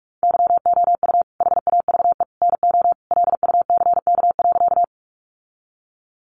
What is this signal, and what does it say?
Morse code, heard as a keyed tone, about 44 wpm, 700 Hz. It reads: YOU HAVE NO LUCK.
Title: Sentences from Top 1500 Words 45wpm